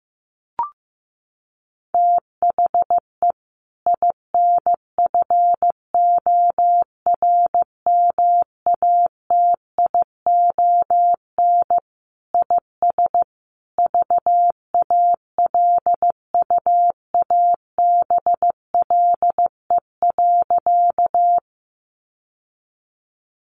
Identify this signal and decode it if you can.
Morse code, heard as a keyed tone, about 14 wpm, 700 Hz. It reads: THE INFORMATION IS VALUABLE.
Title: Sentences from Top 1100 Words 15wpm